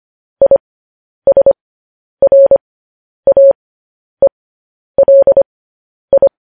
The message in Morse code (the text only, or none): ISRAELI